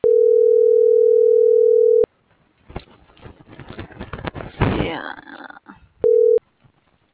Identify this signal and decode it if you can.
Ringback tone (North America), 440 + 480 Hz.
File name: CENTRAL GENERAL / out tania.lemus